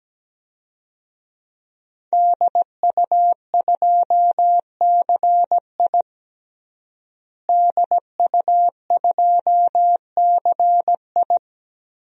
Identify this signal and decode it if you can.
Morse code, heard as a keyed tone, about 17 wpm, 700 Hz. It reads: DU2CI DU2CI
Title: DXCC-100 Call Signs - 5 Characters - Repeated 3x - 17wpm